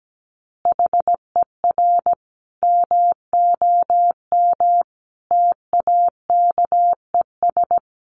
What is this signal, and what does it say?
Morse code, heard as a keyed tone, about 17 wpm, 700 Hz. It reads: HER MOM TAKES
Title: Additional Easy Phrases 17wpm